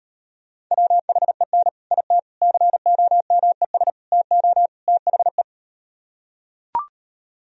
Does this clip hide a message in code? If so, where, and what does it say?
Yes: Morse code from 0.7 to 5.4 s, heard as a keyed tone, about 38 wpm, 700 Hz. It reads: WHEN IT COMES TO THE